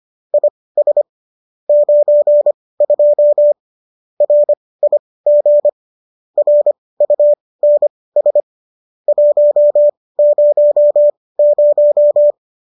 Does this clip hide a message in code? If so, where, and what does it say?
Yes: Morse code from 0.3 to 12.3 s, heard as a keyed tone, about 25 wpm, spaced as for 18 wpm, 590 Hz. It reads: IS 92 RIG RUNS 100